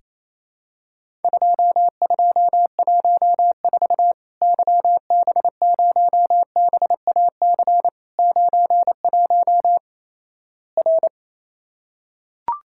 Morse code, heard as a keyed tone, about 28 wpm, 700 Hz. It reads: 2214 YB0BAC 91 R
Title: Worked All Europe DX Contest - QTC Exchange - 28wpm